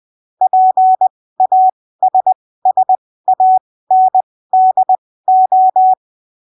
Morse code, heard as a keyed tone, about 20 wpm, spaced as for 15 wpm, 750 Hz. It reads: PASSANDO
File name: arquivos / texto / texto 26 20ppm